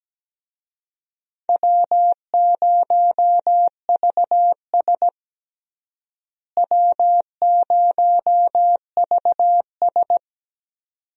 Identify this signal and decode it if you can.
Morse code, heard as a keyed tone, about 17 wpm, 700 Hz. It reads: W0VS W0VS
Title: USA Call Signs - 4 Characters - Repeated 2x - 17wpm